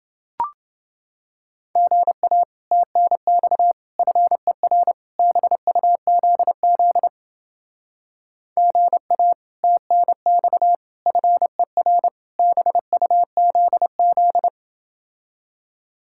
Morse code, heard as a keyed tone, about 30 wpm, 700 Hz. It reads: GA TNX FER BUZZ GA TNX FER BUZZ